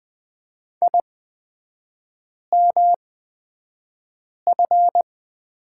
Morse code, heard as a keyed tone, about 20 wpm, spaced as for 5 wpm, 700 Hz. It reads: IMF
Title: 250 Lesson 40e - BK - Three Characters 20wpm